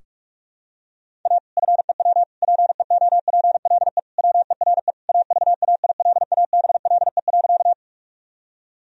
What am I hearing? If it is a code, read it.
Morse code, heard as a keyed tone, about 46 wpm, 700 Hz. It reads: A FEW PEOPLE WERE AVAILABLE.
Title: Sentences from Top 1000 Words - Review Part 3 - 45wpm